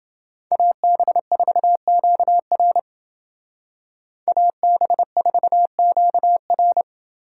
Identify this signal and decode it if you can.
Morse code, heard as a keyed tone, about 29 wpm, 700 Hz. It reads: AB4QR AB4QR